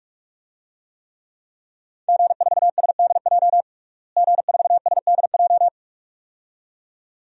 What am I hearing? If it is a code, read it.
Morse code, heard as a keyed tone, about 46 wpm, 700 Hz. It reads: G4SDJ G4SDJ
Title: DXCC-100 Call Signs - 5 Characters - Repeated 2x - 45wpm